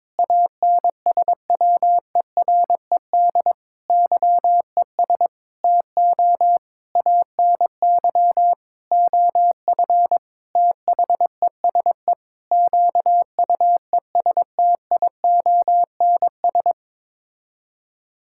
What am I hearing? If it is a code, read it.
Morse code, heard as a keyed tone, about 21 wpm, 700 Hz. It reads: ANSWERED YES TO ANY OF THESE QUESTIONS
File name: Sets of 9 Words - 22wpm